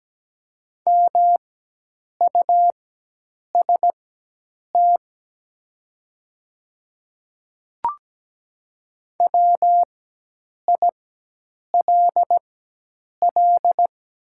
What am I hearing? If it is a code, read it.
Morse code, heard as a keyed tone, about 17 wpm, spaced as for 8 wpm, 700 Hz. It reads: MUST WILL
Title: 4x Character Spacing - Top 200 Words - 17wpm